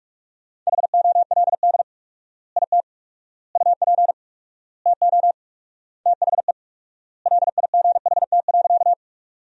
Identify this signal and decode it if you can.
Morse code, heard as a keyed tone, about 46 wpm, 700 Hz. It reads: HOLD IT UP TO THE LIGHT.
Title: 4x Word-Spacing - Sentences from Top 400 Words - 45wpm